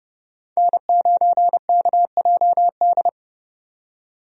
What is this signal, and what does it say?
Morse code, heard as a keyed tone, about 31 wpm, 700 Hz. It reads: N9KJD
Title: USA Call Signs - 5 Characters - 30wpm